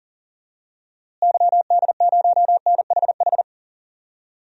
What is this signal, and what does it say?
Morse code, heard as a keyed tone, about 40 wpm, 700 Hz. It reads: YD0NHH
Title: DXCC-100 Call Signs - 6 Characters - Repeated 2x - 40wpm